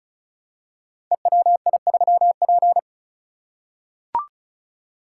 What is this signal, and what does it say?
Morse code, heard as a keyed tone, about 36 wpm, 700 Hz. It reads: EWI3P
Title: Code Groups - 5 Letter:Numbers - 35wpm